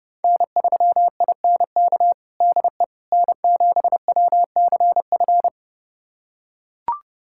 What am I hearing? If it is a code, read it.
Morse code, heard as a keyed tone, about 30 wpm, 700 Hz. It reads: N3INK DE N7WCF
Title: USA Callsign DE Callsign - 5 Characters - 30wpm